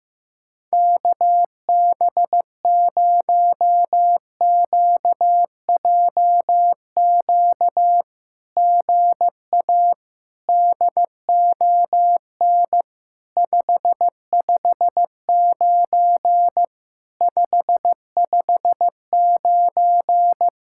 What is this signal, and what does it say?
Morse code, heard as a keyed tone, about 15 wpm, 700 Hz. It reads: KB0QJQ GA DON 559 559
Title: SOTA QSO Practice 15wpm